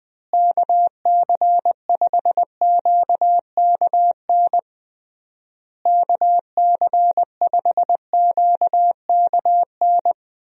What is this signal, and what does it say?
Morse code, heard as a keyed tone, about 21 wpm, 700 Hz. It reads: KC5QKN KC5QKN
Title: USA Call Signs - 6 Characters - Repeated 3x - 20wpm